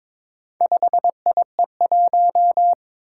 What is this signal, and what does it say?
Morse code, heard as a keyed tone, about 23 wpm, 700 Hz. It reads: 5IE1
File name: practice / 22wpm / Four Letter-Number Combinations - 22wpm